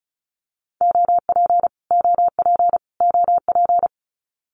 Morse code, heard as a keyed tone, about 34 wpm, 700 Hz. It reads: OP OP OP
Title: Abbreviations Repeated 3x - 35wpm